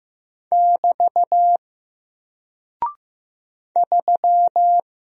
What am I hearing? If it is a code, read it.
Morse code, heard as a keyed tone, about 15 wpm, 700 Hz. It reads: <BT> 3
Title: Single Character (Letter:Number:Punctuation:Prosign) 15wpm